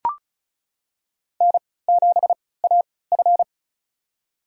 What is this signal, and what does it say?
Morse code, heard as a keyed tone, about 36 wpm, spaced as for 20 wpm, 700 Hz. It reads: N7AF
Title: USA Call Signs - 3x Character Spacing - 35wpm